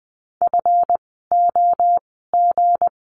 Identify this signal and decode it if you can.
Morse code, heard as a keyed tone, about 20 wpm, spaced as for 15 wpm, 700 Hz. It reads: FOG
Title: 160 Lesson 26f - Letter g - Words 20wpm